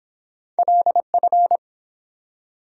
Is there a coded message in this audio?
Yes: Morse code from 0.6 to 1.6 s, heard as a keyed tone, about 30 wpm, 700 Hz. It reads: LF